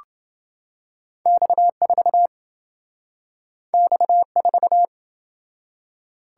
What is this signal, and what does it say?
Morse code, heard as a keyed tone, about 27 wpm, 700 Hz. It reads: X4 X4